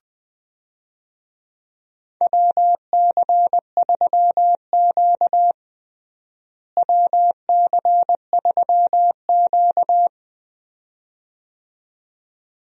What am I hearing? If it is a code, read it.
Morse code, heard as a keyed tone, about 20 wpm, 700 Hz. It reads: WC3Q WC3Q
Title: USA Call Signs - 4 Characters - Repeated 2x - 20wpm